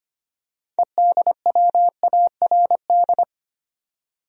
Morse code, heard as a keyed tone, about 25 wpm, 700 Hz. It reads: EDWARD
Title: USA Common Names - 25wpm